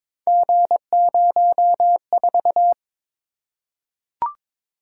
Morse code, heard as a keyed tone, about 22 wpm, 700 Hz. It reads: G04
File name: practice / 22wpm / Three Letter-Number Combinations - 22wpm